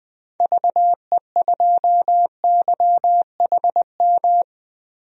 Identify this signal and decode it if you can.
Morse code, heard as a keyed tone, about 20 wpm, 700 Hz. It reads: VE2YHM
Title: DXCC-100 Call Signs - 6 Characters - Repeated 2x - 20wpm